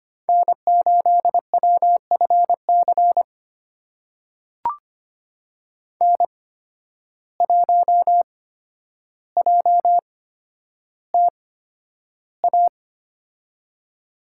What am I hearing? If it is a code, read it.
Morse code, heard as a keyed tone, about 25 wpm, spaced as for 11 wpm, 700 Hz. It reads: N8WFC N1JTA
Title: USA Call Signs - 5 Characters - 8x Character Spacing - 25wpm